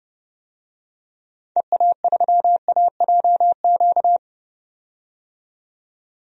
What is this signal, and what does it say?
Morse code, heard as a keyed tone, about 30 wpm, 700 Hz. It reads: EA3AJQ
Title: DXCC-100 Call Signs - 6 Characters - 30wpm